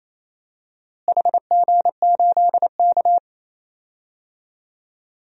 Morse code, heard as a keyed tone, about 27 wpm, 700 Hz. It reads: HG8K
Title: DXCC-100 Call Signs - 4 Characters - Repeated 3x - 28wpm